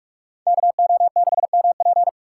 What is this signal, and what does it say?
Morse code, heard as a keyed tone, about 46 wpm, 700 Hz. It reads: KO6MP